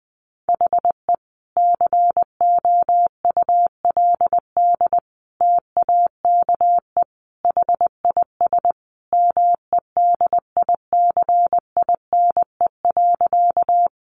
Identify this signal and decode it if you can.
Morse code, heard as a keyed tone, about 20 wpm, 700 Hz. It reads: HE COULD TAKE HIS MEDICINE.
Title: Sentences from Top 900 Words 20wpm